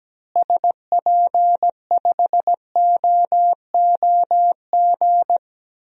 Morse code, heard as a keyed tone, about 17 wpm, 700 Hz. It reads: SP5OOG
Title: DXCC-100 Call Signs - 6 Characters - Repeated 3x - 17wpm